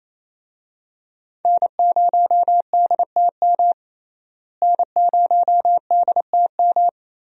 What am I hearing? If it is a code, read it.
Morse code, heard as a keyed tone, about 28 wpm, 700 Hz. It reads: N0DTM N0DTM
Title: USA Call Signs - 5 Characters - Repeated 3x - 28wpm